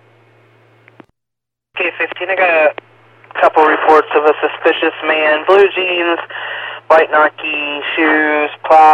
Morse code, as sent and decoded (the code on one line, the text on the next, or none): none